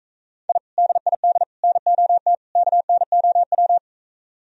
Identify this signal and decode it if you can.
Morse code, heard as a keyed tone, about 45 wpm, 700 Hz. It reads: I DID NOT KNOW